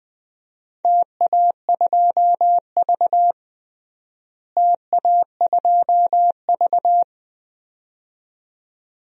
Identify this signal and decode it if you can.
Morse code, heard as a keyed tone, about 20 wpm, 700 Hz. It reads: TA2V TA2V
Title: DXCC-100 Call Signs - 4 Characters - Repeated 2x - 20wpm